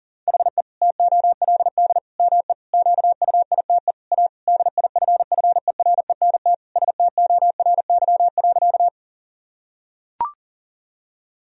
Morse code, heard as a keyed tone, about 40 wpm, 700 Hz. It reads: HE TOLD ME QUITE A DIFFERENT STORY.
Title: Sentences from Top 1000 Words - Review Part 3 - 40wpm